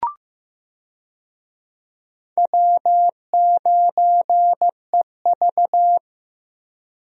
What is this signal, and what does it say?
Morse code, heard as a keyed tone, about 15 wpm, 700 Hz. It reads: W9EV
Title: USA Call Signs - 4 Characters - Repeated 2x - 15wpm